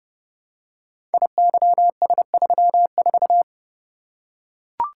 Morse code, heard as a keyed tone, about 30 wpm, 700 Hz. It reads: IYS34
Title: Code Groups - 5 Letter:Numbers - 30wpm